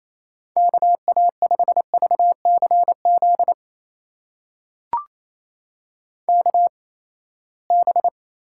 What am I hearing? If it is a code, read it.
Morse code, heard as a keyed tone, about 28 wpm, spaced as for 19 wpm, 700 Hz. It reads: KA5VCZ KB
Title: USA Call Signs - 8x Character Spacing - 28wpm